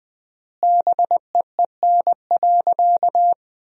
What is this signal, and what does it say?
Morse code, heard as a keyed tone, about 20 wpm, 700 Hz. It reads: BEEN.
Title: 3x Word-Spacing - Sentences from Top 300 Words - 20wpm